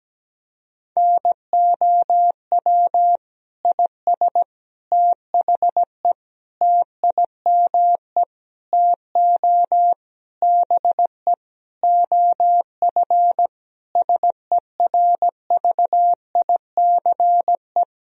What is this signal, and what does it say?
Morse code, heard as a keyed tone, about 17 wpm, 700 Hz. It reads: NOW IS THE TIME TO BE OF SERVICE
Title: Sentences from Top 1000 Words - Review Part 4 - 17wpm